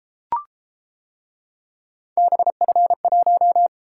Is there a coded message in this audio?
Yes: Morse code from 2.2 to 3.7 s, heard as a keyed tone, about 31 wpm, 700 Hz. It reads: BF1